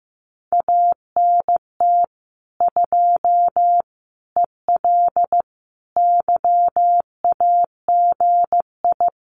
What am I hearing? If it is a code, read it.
Morse code, heard as a keyed tone, about 15 wpm, 700 Hz. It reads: ANT 2 EL YAGI
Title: Common QSO Phrases Encore 15wpm